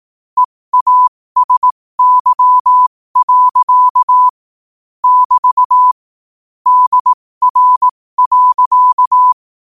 Morse code, heard as a keyed tone, about 18 wpm, spaced as for 15 wpm, 1000 Hz. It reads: EASY. <BT> DR.
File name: qotdfiles / 15wpm